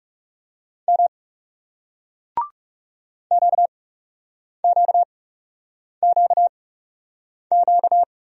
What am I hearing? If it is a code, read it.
Morse code, heard as a keyed tone, about 42 wpm, spaced as for 8 wpm, 700 Hz. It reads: M QQQQ